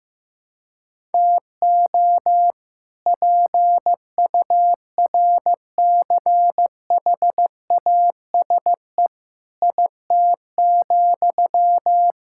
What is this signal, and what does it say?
Morse code, heard as a keyed tone, about 15 wpm, 700 Hz. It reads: TO PURCHASE IT,
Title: Sentences from Top 1000 Words - Review Part 5 - 15wpm